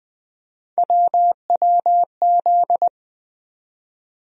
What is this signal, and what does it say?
Morse code, heard as a keyed tone, about 20 wpm, 700 Hz. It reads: WWZ
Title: USA Call Sign Postfixes - 3 Letters 20wpm